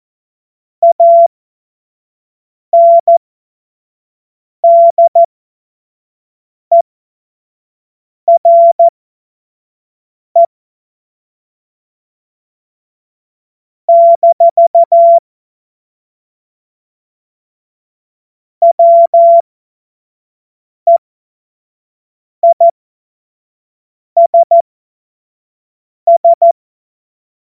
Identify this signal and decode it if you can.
Morse code, heard as a keyed tone, about 14 wpm, spaced as for 5 wpm, 680 Hz. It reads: ANDERE - WEISS
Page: cw-witze